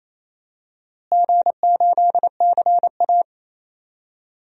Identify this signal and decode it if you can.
Morse code, heard as a keyed tone, about 28 wpm, 700 Hz. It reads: G8CA
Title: DXCC-100 Call Signs - 4 Characters - Repeated 2x - 28wpm